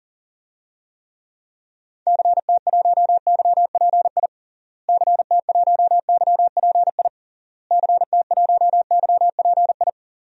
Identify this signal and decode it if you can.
Morse code, heard as a keyed tone, about 40 wpm, 700 Hz. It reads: CT1YPI CT1YPI CT1YPI
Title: DXCC-100 Call Signs - 6 Characters - Repeated 3x - 40wpm